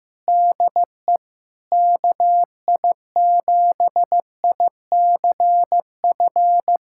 Morse code, heard as a keyed tone, about 15 wpm, 700 Hz. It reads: DE KI7ICF
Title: USA Callsign DE Callsign - 6 Characters - 15wpm